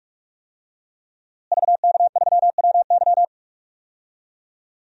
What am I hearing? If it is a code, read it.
Morse code, heard as a keyed tone, about 45 wpm, 700 Hz. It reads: VK3WY
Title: DXCC-100 Call Signs - 5 Characters - Repeated 2x - 45wpm